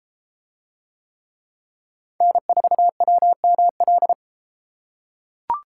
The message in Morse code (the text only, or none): N4WML